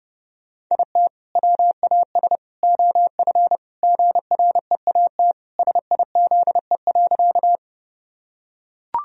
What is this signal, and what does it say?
Morse code, heard as a keyed tone, about 30 wpm, 700 Hz. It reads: IT WAS OF GREAT SIZE.
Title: Sentences from Top 1000 Words - Review Part 2 - 30wpm